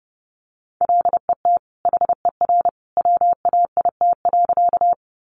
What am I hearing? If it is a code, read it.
Morse code, heard as a keyed tone, about 30 wpm, 700 Hz. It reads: LET HER WAIT.
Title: Sentences from Top 500 Words 30wpm